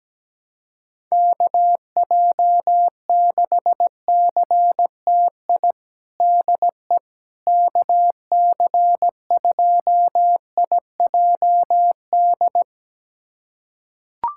Morse code, heard as a keyed tone, about 17 wpm, 700 Hz. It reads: KJ6CTI DE KC2IJD
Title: USA Callsign DE Callsign - 6 Characters - 17wpm